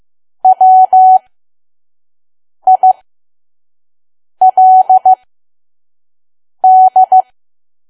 Morse code, heard as a keyed tone, about 15 wpm, spaced as for 5 wpm, 750 Hz. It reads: WILD
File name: Archive / 5 WPM files / PM